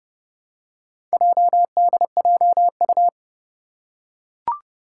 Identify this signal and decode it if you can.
Morse code, heard as a keyed tone, about 30 wpm, 700 Hz. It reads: JDJU
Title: Four Challenging Letter-Number Combinations 30wpm